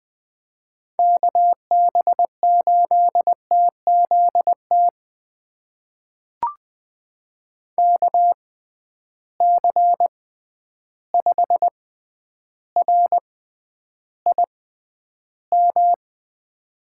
Morse code, heard as a keyed tone, about 20 wpm, spaced as for 10 wpm, 700 Hz. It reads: KB8TZT KC5RIM
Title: USA Call Signs - 6 Characters - 6x Character Spacing - 20wpm